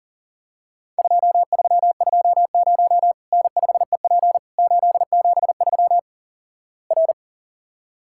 Morse code, heard as a keyed tone, about 37 wpm, 700 Hz. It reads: 2320 N5EP 873 R
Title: Worked All Europe DX Contest - QTC Exchange - 40wpm